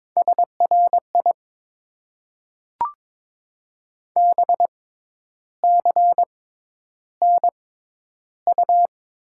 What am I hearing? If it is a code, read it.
Morse code, heard as a keyed tone, about 22 wpm, spaced as for 10 wpm, 700 Hz. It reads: SRI BCNU
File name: Top 90 QSO Elements - 6x Character Spacing - 22wpm